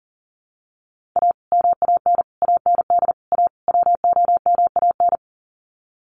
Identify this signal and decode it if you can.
Morse code, heard as a keyed tone, about 40 wpm, 700 Hz. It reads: A MAN AND A WOMAN
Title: Sets of 5 Words 40wpm